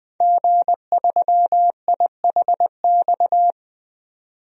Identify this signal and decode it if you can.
Morse code, heard as a keyed tone, about 20 wpm, 700 Hz. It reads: G3IHX
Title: DXCC-100 Call Signs - 5 Characters - 20wpm